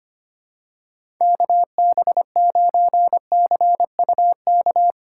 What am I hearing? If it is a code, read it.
Morse code, heard as a keyed tone, about 25 wpm, 700 Hz. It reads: KB9CUK
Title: USA Call Signs - 6 Characters - 2x Character Spacing - 25wpm